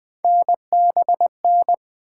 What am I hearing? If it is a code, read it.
Morse code, heard as a keyed tone, about 20 wpm, 700 Hz. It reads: NBN
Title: Three Letter-Number Combinations - 20wpm